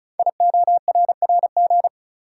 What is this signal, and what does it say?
Morse code, heard as a keyed tone, about 36 wpm, 690 Hz. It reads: IORRG
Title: Code Groups - 5 Letters - 35wpm